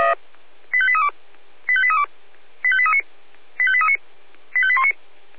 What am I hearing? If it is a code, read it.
Five-tone selective call (ZVEI), heard as quick runs of five tones, tones of about 70 ms. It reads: 85832 85832 85839 85839 85819
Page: ZVEI